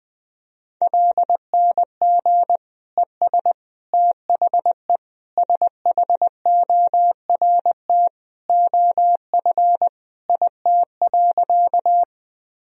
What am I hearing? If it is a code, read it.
Morse code, heard as a keyed tone, about 20 wpm, 700 Hz. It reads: LNG ES THE SHORT OF IT.
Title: Sentences from Top 400 Words and Abbreviations 20wpm